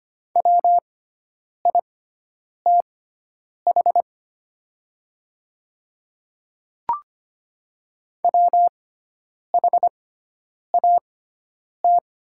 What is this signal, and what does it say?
Morse code, heard as a keyed tone, about 25 wpm, spaced as for 9 wpm, 700 Hz. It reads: WITH WHAT
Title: 6x Character Spacing - Top 200 Words - 25wpm